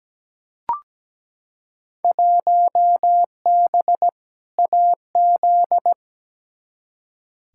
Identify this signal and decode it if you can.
Morse code, heard as a keyed tone, about 17 wpm, 700 Hz. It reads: 1B AZ
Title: Field Day Exchanges Encore - 17wpm